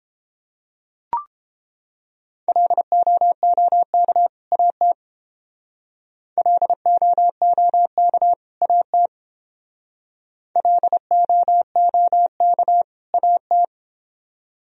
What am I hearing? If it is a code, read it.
Morse code, heard as a keyed tone, about 28 wpm, 700 Hz. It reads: LOOK AT LOOK AT LOOK AT